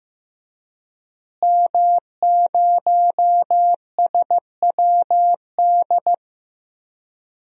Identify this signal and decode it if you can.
Morse code, heard as a keyed tone, about 15 wpm, 700 Hz. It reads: M0SWD
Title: DXCC-100 Call Signs - 5 Characters - Repeated 2x - 15wpm